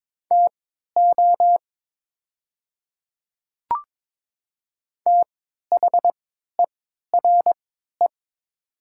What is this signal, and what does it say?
Morse code, heard as a keyed tone, about 21 wpm, spaced as for 12 wpm, 700 Hz. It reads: TO THERE
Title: 3x Character Spacing - Top 100 Words - 22wpm